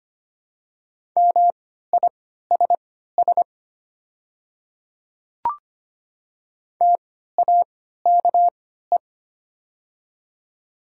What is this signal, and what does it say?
Morse code, heard as a keyed tone, about 25 wpm, spaced as for 14 wpm, 690 Hz. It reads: MISS TAKE
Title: 3x Character Spacing - Top 500 Words - 25wpm